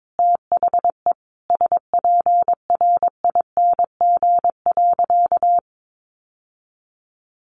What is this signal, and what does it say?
Morse code, heard as a keyed tone, about 22 wpm, 700 Hz. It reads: THE SPRING.
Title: Sentences from Top 1200 Words 22wpm